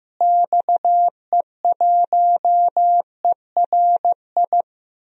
Morse code, heard as a keyed tone, about 16 wpm, 700 Hz. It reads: XE1ERI
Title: DXCC-100 Call Signs - 6 Characters - Repeated 3x - 15wpm